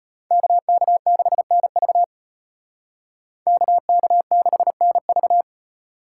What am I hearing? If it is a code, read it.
Morse code, heard as a keyed tone, about 33 wpm, 700 Hz. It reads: KK6NV KK6NV